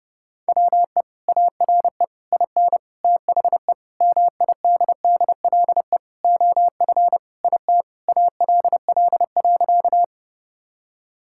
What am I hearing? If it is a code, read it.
Morse code, heard as a keyed tone, about 30 wpm, 700 Hz. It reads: WE ARE IN THE MIDDLE OF IT ALL.
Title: Sentences from Top 1000 Words - Review Part 6 - 30wpm